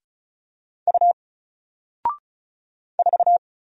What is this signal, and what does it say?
Morse code, heard as a keyed tone, about 36 wpm, 700 Hz. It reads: U 4
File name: Single-Letter-Number-35wpm